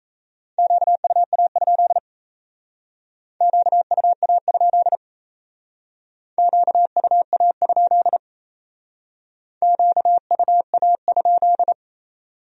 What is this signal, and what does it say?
Morse code, heard as a keyed tone, about 36 wpm, 700 Hz. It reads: QUA? QUA? QUA? QUA?